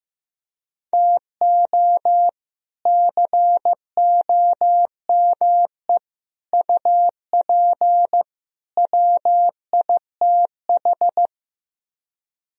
Morse code, heard as a keyed tone, about 15 wpm, 700 Hz. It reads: TO COME UP WITH
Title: Speed-Racing - Sets of 4 Words 15wpm